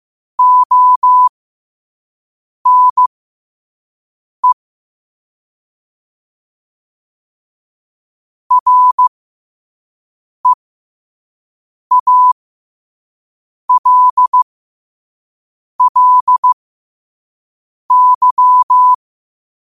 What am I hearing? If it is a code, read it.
Morse code, heard as a keyed tone, about 15 wpm, spaced as for 5 wpm, 1000 Hz. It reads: ONE REALLY
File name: qotdfiles / 5wpm